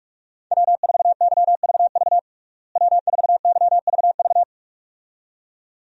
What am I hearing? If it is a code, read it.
Morse code, heard as a keyed tone, about 46 wpm, 700 Hz. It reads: W4YVV W4YVV
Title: USA Call Signs - 5 Characters - Repeated 2x - 45wpm